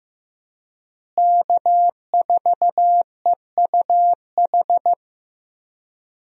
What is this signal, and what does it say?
Morse code, heard as a keyed tone, about 14 wpm, 700 Hz. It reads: K4EUH
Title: USA Call Signs - 5 Characters - Repeated 3x - 15wpm